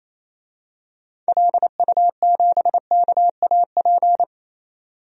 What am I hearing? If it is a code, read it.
Morse code, heard as a keyed tone, about 28 wpm, 700 Hz. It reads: LU7KAP
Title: DXCC-100 Call Signs - 6 Characters - 28wpm